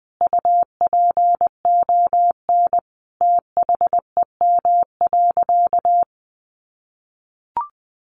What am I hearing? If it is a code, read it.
Morse code, heard as a keyed tone, about 20 wpm, 700 Hz. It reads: UPON THEM.
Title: Sentences from Top 1200 Words 20wpm